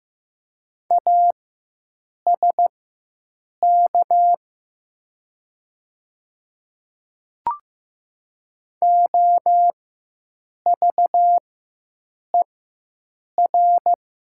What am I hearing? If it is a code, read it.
Morse code, heard as a keyed tone, about 15 wpm, spaced as for 7 wpm, 700 Hz. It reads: ASK OVER
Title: 4x Character Spacing - Top 300 Words - 15wpm